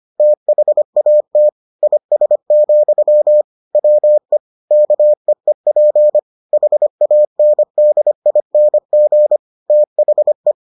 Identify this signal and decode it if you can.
Morse code, heard as a keyed tone, about 25 wpm, 600 Hz. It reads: THAT IS, WE KEEP HANDING THE